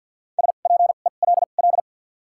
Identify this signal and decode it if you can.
Morse code, heard as a keyed tone, about 53 wpm, spaced as for 37 wpm, 700 Hz. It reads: SPELL
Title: 2x Character Spacing - Top 300 Words - 50wpm